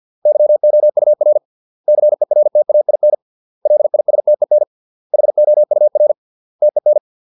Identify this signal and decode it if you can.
Morse code, heard as a keyed tone, about 50 wpm, 600 Hz. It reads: YOUR CERTAIN LISTEN HOUR TEN